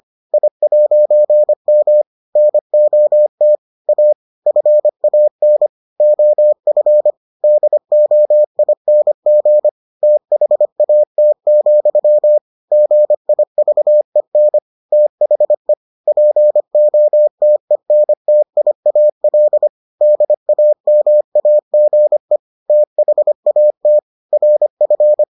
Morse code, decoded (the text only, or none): I'M NOT A FAN OF DOING THAT, GIVEN THE POTENTIAL DAMAGE THAT RF